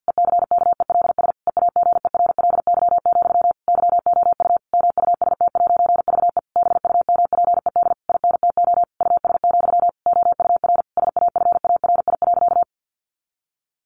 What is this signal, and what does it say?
Morse code, heard as a keyed tone, about 50 wpm, 700 Hz. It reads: ECKELS EAGERLY, YOU MUST'VE BUMPED INTO US, OUR SAFARI.
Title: A Sound Of Thunder 50wpm